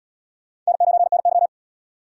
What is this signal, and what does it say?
Morse code, heard as a keyed tone, about 78 wpm, 700 Hz. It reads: N9NY